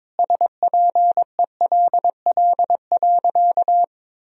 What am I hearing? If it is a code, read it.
Morse code, heard as a keyed tone, about 22 wpm, 700 Hz. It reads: SPELL.
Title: Sentences from Top 1000 Words - Review Part 1 - 22wpm